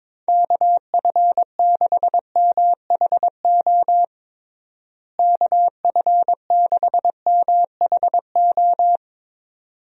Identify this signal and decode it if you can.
Morse code, heard as a keyed tone, about 22 wpm, 700 Hz. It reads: KF6MHO KF6MHO
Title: USA Call Signs - 6 Characters - Repeated 2x - 22wpm